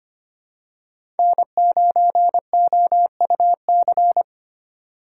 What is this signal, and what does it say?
Morse code, heard as a keyed tone, about 25 wpm, 700 Hz. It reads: N9OUC
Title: USA Call Signs - 5 Characters - 2x Character Spacing - 25wpm